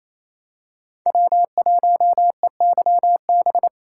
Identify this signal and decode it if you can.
Morse code, heard as a keyed tone, about 27 wpm, 700 Hz. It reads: W1EYB
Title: USA Call Signs - 4x Character Spacing - 28wpm